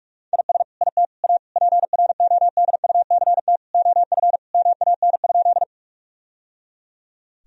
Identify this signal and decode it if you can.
Morse code, heard as a keyed tone, about 43 wpm, 700 Hz. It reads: IS IT A PRODUCT OF MAN?
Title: Sentences from Top 1000 Words - Review Part 2 - 45wpm